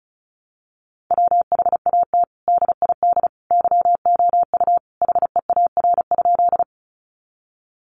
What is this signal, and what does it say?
Morse code, heard as a keyed tone, about 34 wpm, 700 Hz. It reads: WHAT DID YOU HEAR?
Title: Sentences from Top 300 Words 35wpm